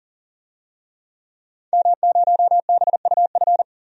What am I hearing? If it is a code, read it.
Morse code, heard as a keyed tone, about 40 wpm, 700 Hz. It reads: M0BUF
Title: DXCC-100 Call Signs - 5 Characters - 40wpm